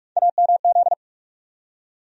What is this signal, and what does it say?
Morse code, heard as a keyed tone, about 43 wpm, 700 Hz. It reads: AMZ
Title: USA Call Sign Postfixes - 3 Letters 45wpm